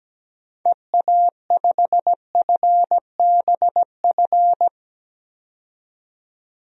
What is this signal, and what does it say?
Morse code, heard as a keyed tone, about 17 wpm, 700 Hz. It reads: EA5FBF
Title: DXCC-100 Call Signs - 6 Characters - 17wpm